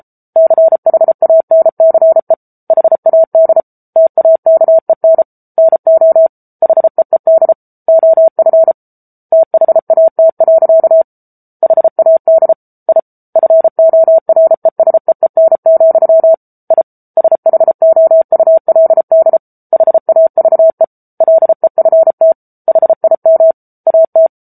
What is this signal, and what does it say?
Morse code, heard as a keyed tone, about 34 wpm, 650 Hz. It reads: CHANCE HAD TAKEN NO HEED OF THAT. HAD I FORESEEN, I SHOULD HAVE LEFT HIM AT
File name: Book Two / 07-Chapter 7 - 33 WPM